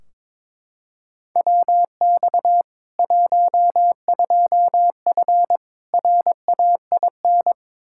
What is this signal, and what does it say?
Morse code, heard as a keyed tone, about 22 wpm, 700 Hz. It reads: WX 12F RAIN
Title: K1USN SST QSO Practice 22wpm